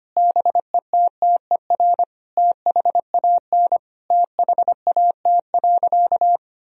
Morse code, heard as a keyed tone, about 25 wpm, 700 Hz. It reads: BETTER THAN THAT.
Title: Sentences from Top 1000 Words - Review Part 4 - 25wpm